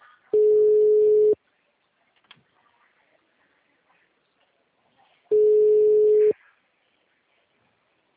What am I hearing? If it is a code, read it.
Ringback tone (Europe, 425 Hz).
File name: out- -2017- - - .67862